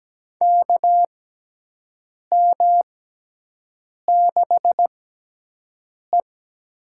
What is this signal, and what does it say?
Morse code, heard as a keyed tone, about 17 wpm, spaced as for 6 wpm, 700 Hz. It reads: KM6E
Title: USA Call Signs - 6x Character Spacing - 17wpm